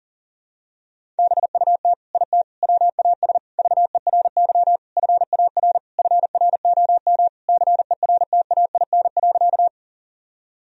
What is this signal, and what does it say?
Morse code, heard as a keyed tone, about 40 wpm, 700 Hz. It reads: BUT IT WAS VERY FAR FROM CERTAIN.
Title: Sentences from Top 1000 Words - Review Part 2 - 40wpm